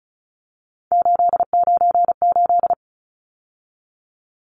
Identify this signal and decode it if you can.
Morse code, heard as a keyed tone, about 34 wpm, 700 Hz. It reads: 898
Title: Numbers - 3-Digit 35wpm